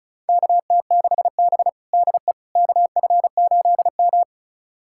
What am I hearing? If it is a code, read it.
Morse code, heard as a keyed tone, about 36 wpm, 700 Hz. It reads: KT6B DE KF8M
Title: USA Callsign DE Callsign - 4 Characters - 35wpm